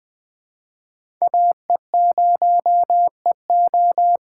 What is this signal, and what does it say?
Morse code, heard as a keyed tone, about 21 wpm, 700 Hz. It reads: AE0EO
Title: USA Call Signs - 5 Characters - 2x Character Spacing - 20wpm